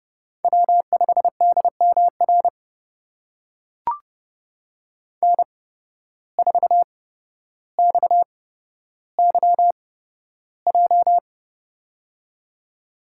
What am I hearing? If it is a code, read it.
Morse code, heard as a keyed tone, about 32 wpm, spaced as for 13 wpm, 700 Hz. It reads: W5DMR N4XYJ
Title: USA Call Signs - 5 Characters - 8x Character Spacing - 30wpm